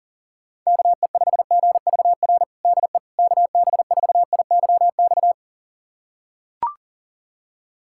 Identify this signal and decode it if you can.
Morse code, heard as a keyed tone, about 40 wpm, 700 Hz. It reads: KE5GVR DE KB4IYX
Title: USA Callsign DE Callsign 40wpm